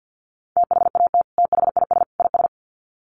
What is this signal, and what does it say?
Morse code, heard as a keyed tone, about 48 wpm, 700 Hz. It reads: THAT THIS IS
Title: Sets of 3 Words 50wpm